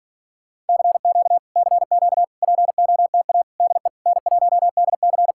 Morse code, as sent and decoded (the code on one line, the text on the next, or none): -.-. --.- / -.-. --.- / .--. --- - .- / -.. / -. .---- -.. -.-.
CQ CQ POTA DE N1DC